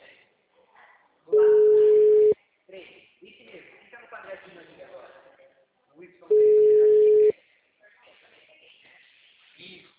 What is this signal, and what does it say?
Ringback tone (Europe, 425 Hz), with speech.
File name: out- -2005- - - .82339